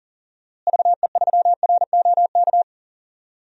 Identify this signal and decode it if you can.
Morse code, heard as a keyed tone, about 40 wpm, 700 Hz. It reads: VE3ROK
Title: DXCC-100 Call Signs - 6 Characters - Repeated 2x - 40wpm